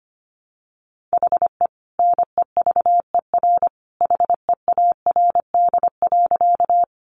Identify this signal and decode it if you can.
Morse code, heard as a keyed tone, about 25 wpm, 700 Hz. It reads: HE NEVER HEARD.
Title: Sentences from Top 400 Words 25wpm